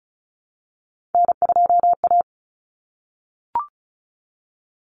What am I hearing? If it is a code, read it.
Morse code, heard as a keyed tone, about 34 wpm, 700 Hz. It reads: N2A
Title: Common Cut Numbers - 3-Digit 35wpm